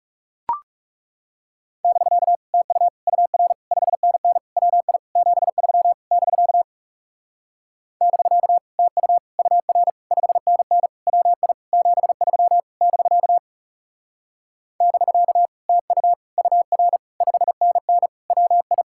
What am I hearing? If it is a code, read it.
Morse code, heard as a keyed tone, about 41 wpm, 700 Hz. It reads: <BK> TU UR 5NN WI 73 <BK> <BK> TU UR 5NN WI 73 <BK> <BK> TU UR 5NN WI